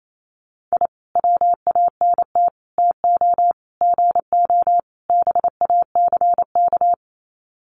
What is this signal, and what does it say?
Morse code, heard as a keyed tone, about 28 wpm, 700 Hz. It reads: I WANT TO GO BACK
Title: Sets of 5 Words Encore 28wpm